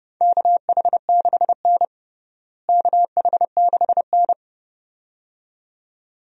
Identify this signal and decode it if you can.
Morse code, heard as a keyed tone, about 30 wpm, 700 Hz. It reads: KH6N KH6N
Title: DXCC-100 Call Signs - 4 Characters - Repeated 2x - 30wpm